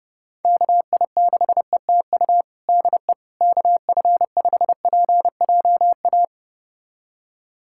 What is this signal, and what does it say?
Morse code, heard as a keyed tone, about 32 wpm, 700 Hz. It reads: KI6ETU DE KF5PJA
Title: USA Callsign DE Callsign - 6 Characters - 30wpm